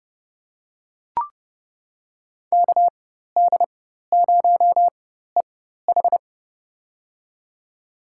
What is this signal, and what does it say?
Morse code, heard as a keyed tone, about 30 wpm, spaced as for 14 wpm, 700 Hz. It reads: KD0EH
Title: USA Call Signs - 5 Characters - 4x Character Spacing - 30wpm